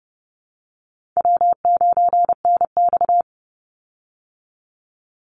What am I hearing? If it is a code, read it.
Morse code, heard as a keyed tone, about 30 wpm, 700 Hz. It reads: W9NX
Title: Easy Call Signs Repeated 3x - 30wpm